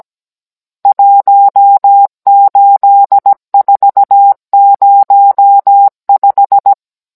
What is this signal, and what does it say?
Morse code, heard as a keyed tone, about 17 wpm, 800 Hz. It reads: 18405
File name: Cours-NR18-16mm